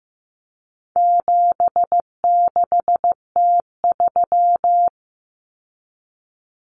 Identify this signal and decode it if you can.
Morse code, heard as a keyed tone, about 15 wpm, 700 Hz. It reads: 76T3
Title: Common Cut Numbers - 4-Digit 15wpm